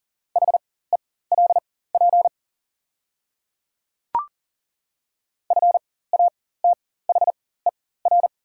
Morse code, heard as a keyed tone, about 42 wpm, spaced as for 19 wpm, 690 Hz. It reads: HELP FATHER
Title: 4x Character Spacing - Top 200 Words - 40wpm